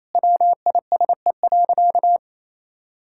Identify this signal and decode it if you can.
Morse code, heard as a keyed tone, about 28 wpm, 700 Hz. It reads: WISE.
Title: Sentences from Top 1600 Words 28wpm